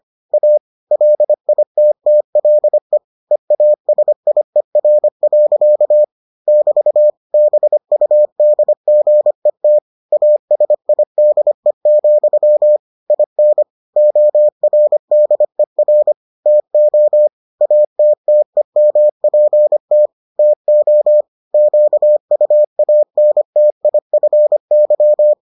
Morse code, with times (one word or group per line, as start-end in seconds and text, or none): 0.3-0.6 A
0.9-3.0 LITTLE
3.3-6.0 EASIER.
6.5-7.1 <BT>
7.3-9.8 BUDGET
10.1-12.8 ASIDE,
13.1-13.6 IN
14.0-16.1 ORDER
16.5-17.3 TO
17.6-20.1 ATTEMPT
20.4-21.2 TO
21.5-25.3 QUANTIFY